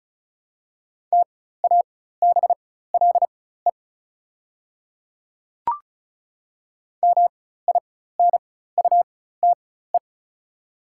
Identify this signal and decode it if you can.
Morse code, heard as a keyed tone, about 36 wpm, spaced as for 16 wpm, 700 Hz. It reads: TABLE MINUTE
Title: 4x Character Spacing - Top 500 Words - 35wpm